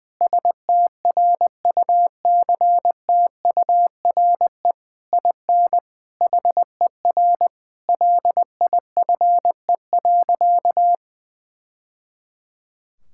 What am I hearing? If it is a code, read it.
Morse code, heard as a keyed tone, about 20 wpm, 700 Hz. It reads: STRUCTURE IN HER LIFE.
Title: Sentences from Top 1000 Words - Review Part 6 - 20wpm